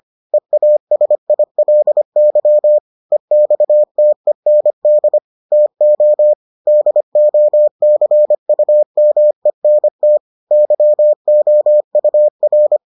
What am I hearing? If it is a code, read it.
Morse code, heard as a keyed tone, about 25 wpm, 600 Hz. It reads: EASILY EXTEND TO DOCUMENT YOUR